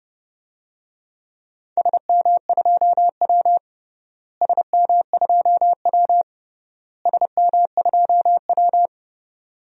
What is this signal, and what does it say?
Morse code, heard as a keyed tone, about 30 wpm, 700 Hz. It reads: SM2W SM2W SM2W
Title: DXCC-100 Call Signs - 4 Characters - Repeated 3x - 30wpm